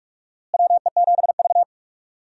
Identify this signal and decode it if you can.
Morse code, heard as a keyed tone, about 46 wpm, 700 Hz. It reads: WE7V
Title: USA Call Signs - 4 Characters - Repeated 2x - 45wpm